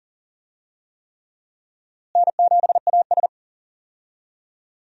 Morse code, heard as a keyed tone, about 40 wpm, 700 Hz. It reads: N7AS
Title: USA Call Signs - 4 Characters - Repeated 3x - 40wpm